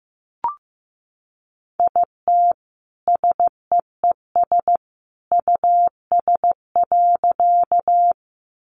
Morse code, heard as a keyed tone, about 15 wpm, 700 Hz. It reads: IT SEES US.
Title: A Sound Of Thunder 15wpm